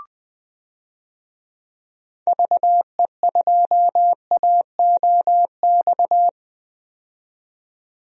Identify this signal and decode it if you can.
Morse code, heard as a keyed tone, about 20 wpm, 700 Hz. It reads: VE2AOX
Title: DXCC-100 Call Signs - 6 Characters - 20wpm